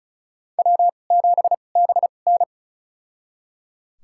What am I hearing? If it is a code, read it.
Morse code, heard as a keyed tone, about 36 wpm, spaced as for 25 wpm, 700 Hz. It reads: W7BN
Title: USA Call Signs - 4 Characters - 2x Character Spacing - 35wpm